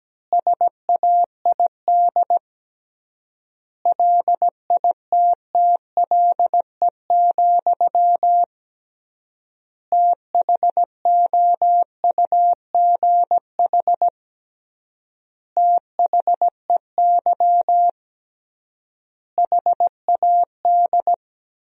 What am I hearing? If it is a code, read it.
Morse code, heard as a keyed tone, about 17 wpm, 700 Hz. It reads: SAID LITTLE, THOUGH THEY HAD
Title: 3x Word-Spacing - Sentences from Top 400 Words - 17wpm